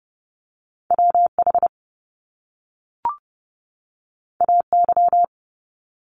Morse code, heard as a keyed tone, about 30 wpm, 700 Hz. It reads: W H A Y
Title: Top Two Letter Combinations in Words 30wpm